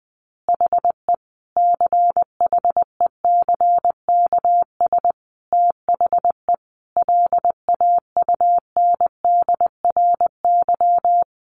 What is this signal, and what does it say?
Morse code, heard as a keyed tone, about 20 wpm, 700 Hz. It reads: HE CHECKS THE LAUNDRY
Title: Easy Phrases 20wpm